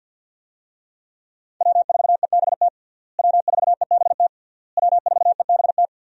Morse code, heard as a keyed tone, about 49 wpm, 700 Hz. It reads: W4EBT W4EBT W4EBT
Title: USA Call Signs - 5 Characters - Repeated 3x - 50wpm